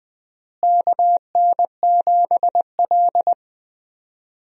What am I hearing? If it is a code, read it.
Morse code, heard as a keyed tone, about 20 wpm, 700 Hz. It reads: KN7L
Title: USA Call Signs - 4 Characters - Repeated 3x - 20wpm